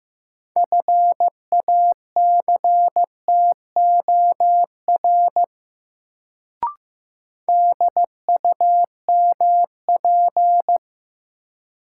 Morse code, heard as a keyed tone, about 15 wpm, 700 Hz. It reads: FACTOR DUMP
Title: Sentences from Top 1300 Words - New Word Drill - 15wpm